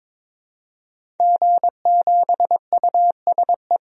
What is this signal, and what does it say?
Morse code, heard as a keyed tone, about 21 wpm, 700 Hz. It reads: G7USE
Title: DXCC-100 Call Signs - 5 Characters - 22wpm